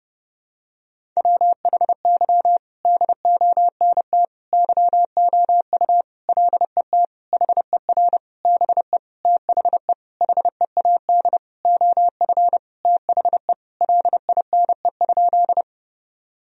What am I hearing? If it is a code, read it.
Morse code, heard as a keyed tone, about 30 wpm, 700 Hz. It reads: WHY DONT YOU LET HER BE THE HEAD OF THE LINE?
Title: Sentences from Top 1000 Words - Review Part 1 - 30wpm